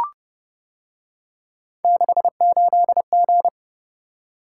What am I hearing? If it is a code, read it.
Morse code, heard as a keyed tone, about 29 wpm, 700 Hz. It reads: 68G